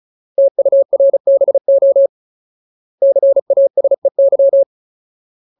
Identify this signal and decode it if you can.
Morse code, heard as a keyed tone, about 34 wpm, 550 Hz. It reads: TURBO CASEY
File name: Ultimate Word List / 35WPM Ultimate